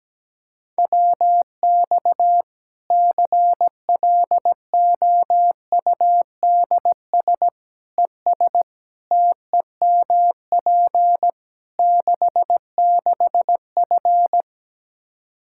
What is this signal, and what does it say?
Morse code, heard as a keyed tone, about 17 wpm, 700 Hz. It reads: WX CLOUDS ES TEMP 66F
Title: Practice QSO - Exchange 2 - Info - 17wpm